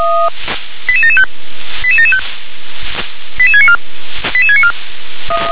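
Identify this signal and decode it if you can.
Five-tone selective call (ZVEI), heard as quick runs of five tones, tones of about 70 ms. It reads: 88775 88775 88694 88694
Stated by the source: ZVEI five-tone selective call received on an SDR.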